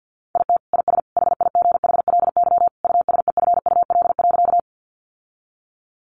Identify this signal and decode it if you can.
Morse code, heard as a keyed tone, about 48 wpm, 700 Hz. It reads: IT IS HIGHLY USEFUL.
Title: Sentences from Top 1100 Words 50wpm